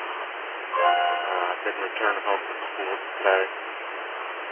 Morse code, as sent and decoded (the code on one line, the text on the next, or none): none